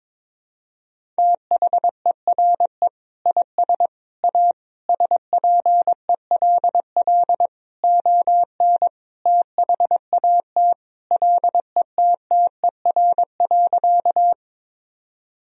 Morse code, heard as a keyed tone, about 22 wpm, 700 Hz. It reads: THERE IS A SPELL ON THAT LETTER.
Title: Sentences from Top 1000 Words - Review Part 1 - 22wpm